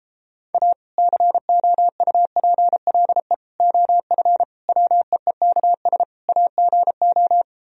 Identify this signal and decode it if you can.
Morse code, heard as a keyed tone, about 33 wpm, 700 Hz. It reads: A COUPLE OF WEEKS AGO